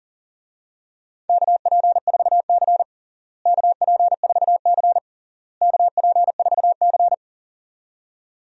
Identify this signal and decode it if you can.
Morse code, heard as a keyed tone, about 40 wpm, 700 Hz. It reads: KP4C KP4C KP4C
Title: DXCC-100 Call Signs - 4 Characters - Repeated 3x - 40wpm